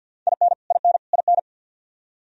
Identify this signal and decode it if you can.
Morse code, heard as a keyed tone, about 50 wpm, 700 Hz. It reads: IN IN IN
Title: Top 100 Words Repeated 50wpm